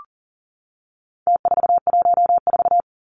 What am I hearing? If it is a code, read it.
Morse code, heard as a keyed tone, about 40 wpm, 700 Hz. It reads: T414
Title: Numbers - Contest Style - 4-Digit 40wpm